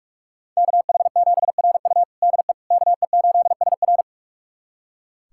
Morse code, heard as a keyed tone, about 46 wpm, 700 Hz. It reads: KH7RU DE KE8SR
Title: USA Callsign DE Callsign - 5 Characters - 45wpm